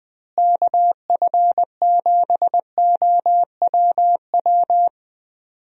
Morse code, heard as a keyed tone, about 20 wpm, 700 Hz. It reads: KF7OWW
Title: USA Call Signs - 6 Characters - Repeated 3x - 20wpm